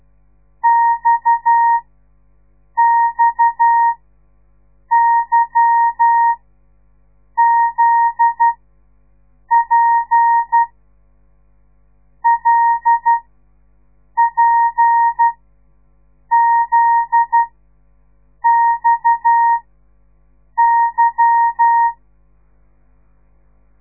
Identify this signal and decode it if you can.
Morse code, heard as a keyed tone, about 12 wpm, spaced as for 6 wpm, 930 Hz. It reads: XXYZP LPZXY
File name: radiomorningstar / morse / les 8